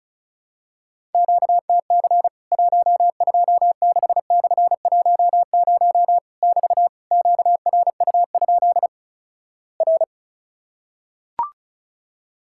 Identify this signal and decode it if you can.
Morse code, heard as a keyed tone, about 36 wpm, 700 Hz. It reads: QTC 126/10 <BT> QRU? R
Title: Worked All Europe DX Contest - QTC Exchange - 35wpm